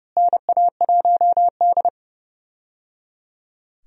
Morse code, heard as a keyed tone, about 30 wpm, 700 Hz. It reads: NA1D
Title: USA Call Signs - 4 Characters - Repeated 3x - 30wpm